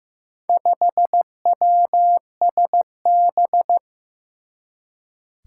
Morse code, heard as a keyed tone, about 15 wpm, 700 Hz. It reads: 5WSB
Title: Four Challenging Letter-Number Combinations 15wpm